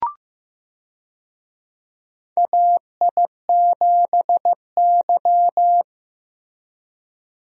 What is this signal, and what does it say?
Morse code, heard as a keyed tone, about 16 wpm, 700 Hz. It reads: AI7Y
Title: USA Call Signs - 4 Characters - Repeated 3x - 15wpm